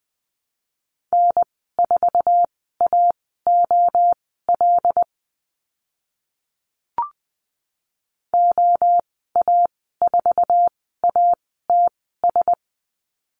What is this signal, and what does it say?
Morse code, heard as a keyed tone, about 20 wpm, spaced as for 14 wpm, 700 Hz. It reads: N4AOL OA4ATS
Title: 082 Lesson 14g - Number 2 - Call Signs 20wpm